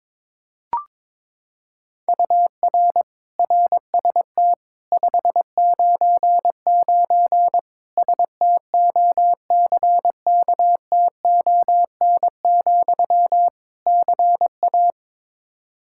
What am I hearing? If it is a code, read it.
Morse code, heard as a keyed tone, about 22 wpm, 700 Hz. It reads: UR RST 599 STOCKTON, CA
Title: K1USN SST QSO Practice 22wpm